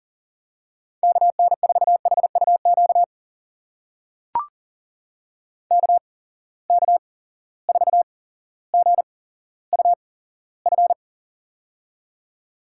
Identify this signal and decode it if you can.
Morse code, heard as a keyed tone, about 40 wpm, spaced as for 17 wpm, 700 Hz. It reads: KN4HUQ KK4GUF
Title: USA Call Signs - 6 Characters - 8x Character Spacing - 40wpm